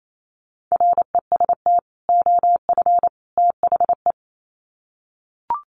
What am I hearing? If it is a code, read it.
Morse code, heard as a keyed tone, about 28 wpm, 700 Hz. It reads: REST OF THE
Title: Sets of 3 Words 28wpm